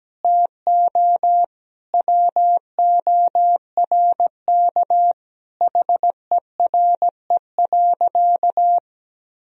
Morse code, heard as a keyed tone, about 17 wpm, 700 Hz. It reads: TO WORK HERE.
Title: Sentences from Top 1000 Words - Review Part 4 - 17wpm